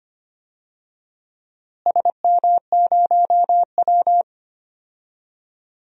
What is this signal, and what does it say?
Morse code, heard as a keyed tone, about 25 wpm, 700 Hz. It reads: SM0W
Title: DXCC-100 Call Signs - 4 Characters - Repeated 2x - 25wpm